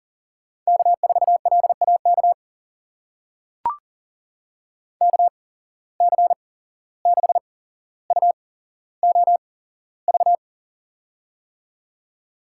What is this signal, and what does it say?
Morse code, heard as a keyed tone, about 40 wpm, spaced as for 16 wpm, 700 Hz. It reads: K4LAK KC6UOV
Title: USA Call Signs - 8x Character Spacing - 40wpm